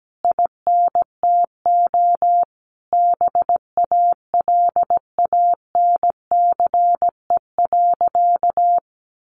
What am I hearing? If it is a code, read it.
Morse code, heard as a keyed tone, about 17 wpm, 700 Hz. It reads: INTO BALANCE.
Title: Sentences from Top 1200 Words 17wpm